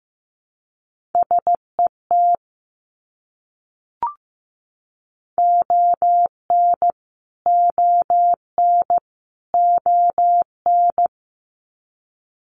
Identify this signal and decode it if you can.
Morse code, heard as a keyed tone, about 15 wpm, 700 Hz. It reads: SET ON ON ON
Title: Top 300 Words Repeated 15wpm